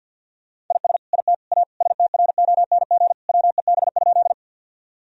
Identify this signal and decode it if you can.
Morse code, heard as a keyed tone, about 50 wpm, 700 Hz. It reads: IS IT A STRONG WEB?
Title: Sentences from Top 1000 Words - Review Part 4 - 50wpm